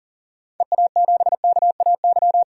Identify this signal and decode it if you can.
Morse code, heard as a keyed tone, about 40 wpm, 700 Hz. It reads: EA7KAY
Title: DXCC-100 Call Signs - 6 Characters - 40wpm